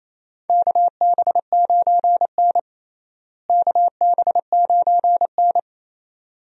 Morse code, heard as a keyed tone, about 28 wpm, 700 Hz. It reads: KB9N KB9N
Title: USA Call Signs - 4 Characters - Repeated 2x - 28wpm